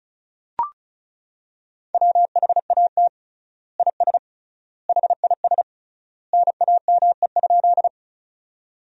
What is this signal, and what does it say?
Morse code, heard as a keyed tone, about 36 wpm, 700 Hz. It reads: WHAT IS HIS NAME?
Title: 3x Word-Spacing - Sentences from Top 200 Words - 35wpm